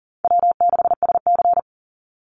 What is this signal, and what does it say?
Morse code, heard as a keyed tone, about 40 wpm, 700 Hz. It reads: W6SC
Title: Easy Call Signs Repeated 2x - 40wpm